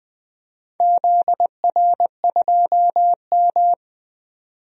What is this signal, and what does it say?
Morse code, heard as a keyed tone, about 20 wpm, 700 Hz. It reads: ZR2M
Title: DXCC-100 Call Signs - 4 Characters - 20wpm